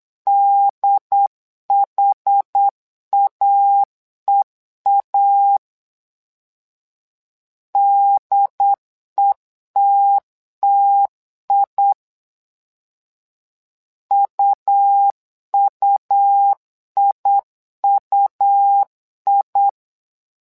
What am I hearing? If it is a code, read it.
Morse code, heard as a keyed tone, about 8 wpm, 800 Hz. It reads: DHAEA DETTI UUIUI
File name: LEZIONI / 2 UHDN / 2 UHD 8wpm 2258